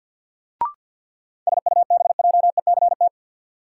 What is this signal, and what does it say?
Morse code, heard as a keyed tone, about 53 wpm, 700 Hz. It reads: SUBJECT
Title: Sentences from Top 800 Words - New Word Drill - 50wpm